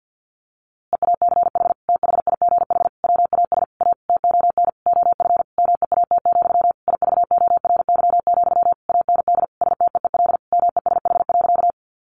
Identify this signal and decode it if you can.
Morse code, heard as a keyed tone, about 50 wpm, 700 Hz. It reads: EACH THIGH WAS A TON OF MEAT, IVORY, AND STEEL MESH.
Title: A Sound Of Thunder 50wpm